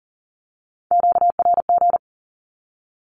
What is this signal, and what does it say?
Morse code, heard as a keyed tone, about 41 wpm, 700 Hz. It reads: QRG
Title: Q-Codes with Spoken Meaning - 40wpm